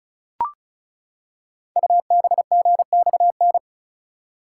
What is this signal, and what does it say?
Morse code, heard as a keyed tone, about 34 wpm, 690 Hz. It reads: UBGXN
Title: Code Groups - 5 Letters - 35wpm